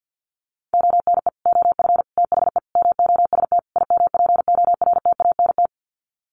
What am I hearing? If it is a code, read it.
Morse code, heard as a keyed tone, about 50 wpm, 700 Hz. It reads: ONE OF THE MOST IMPORTANT
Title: Sets of 5 Words 50wpm